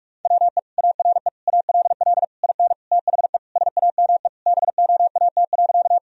Morse code, heard as a keyed tone, about 46 wpm, 700 Hz. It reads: WE ARE ALL IN THE SAME BOAT.
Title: Sentences from Top 1000 Words - Review Part 3 - 45wpm